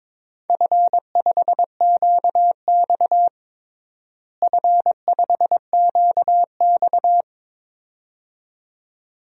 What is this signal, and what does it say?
Morse code, heard as a keyed tone, about 22 wpm, 700 Hz. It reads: F5QX F5QX
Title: DXCC-100 Call Signs - 4 Characters - Repeated 2x - 22wpm